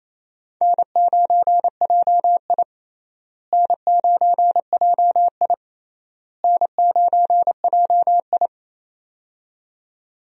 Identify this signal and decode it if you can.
Morse code, heard as a keyed tone, about 28 wpm, 700 Hz. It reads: N9JI N9JI N9JI
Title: USA Call Signs - 4 Characters - Repeated 3x - 28wpm